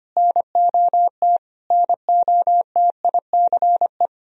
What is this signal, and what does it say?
Morse code, heard as a keyed tone, about 25 wpm, 700 Hz. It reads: NOT NOTICE
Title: Sentences from Top 1000 Words - Review Part 2 - 25wpm